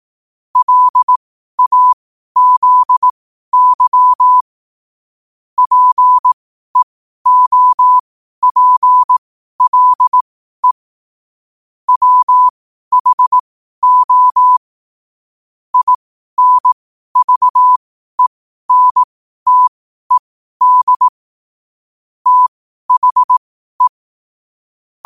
Morse code, heard as a keyed tone, about 18 wpm, spaced as for 12 wpm, 1000 Hz. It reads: LAZY PEOPLE WHO INVENTED THE
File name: qotdfiles / 12wpm